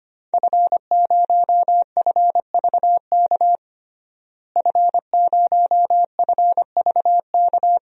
Morse code, heard as a keyed tone, about 25 wpm, 700 Hz. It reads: F0FVK F0FVK
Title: DXCC-100 Call Signs - 5 Characters - Repeated 3x - 25wpm